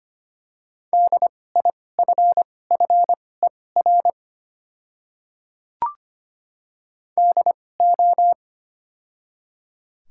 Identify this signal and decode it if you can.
Morse code, heard as a keyed tone, about 25 wpm, spaced as for 18 wpm, 700 Hz. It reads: DIFFER DO
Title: 2x Character Spacing - Top 300 Words - 25wpm